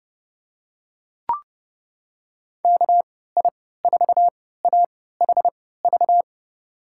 Morse code, heard as a keyed tone, about 30 wpm, spaced as for 17 wpm, 700 Hz. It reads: KI4AHV
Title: USA Call Signs - 6 Characters - 3x Character Spacing - 30wpm